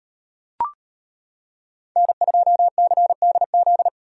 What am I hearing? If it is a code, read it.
Morse code, heard as a keyed tone, about 31 wpm, 700 Hz. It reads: N2CDZ
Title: Speed Racing - USA 1x3 Call Signs 25wpm